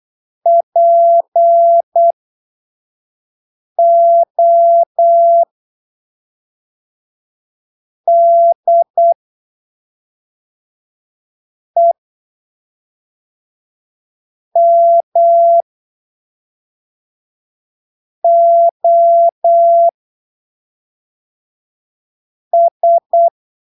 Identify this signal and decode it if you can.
Morse code, heard as a keyed tone, about 8 wpm, spaced as for 3 wpm, 680 Hz. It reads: PODEMOS
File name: arquivos / texto / texto 04 08ppm